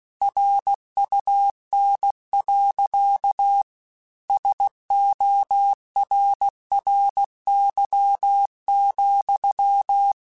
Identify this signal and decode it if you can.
Morse code, heard as a keyed tone, about 16 wpm, 800 Hz. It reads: RUN. SORRY,
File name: QSOs/QSOs-mp3s/kc7haa@15